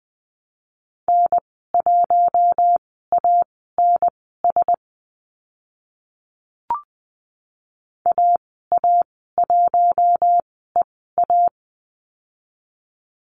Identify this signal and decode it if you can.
Morse code, heard as a keyed tone, about 20 wpm, spaced as for 14 wpm, 700 Hz. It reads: N1ANS AA1EA
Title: 042 Lesson 8g - Number 1 - Call Signs 20wpm